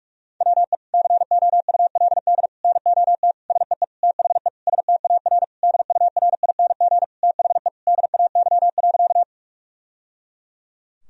Morse code, heard as a keyed tone, about 41 wpm, 700 Hz. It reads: WE COULD NOT SEE THE STAR DURING THE DAY.
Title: Sentences from Top 1000 Words - Review Part 2 - 45wpm